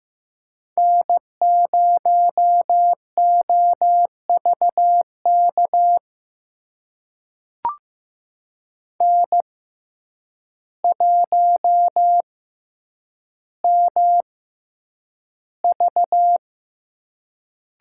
Morse code, heard as a keyed tone, about 15 wpm, spaced as for 8 wpm, 700 Hz. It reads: N0OVK N1MV
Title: USA Call Signs - 5 Characters - 6x Character Spacing - 15wpm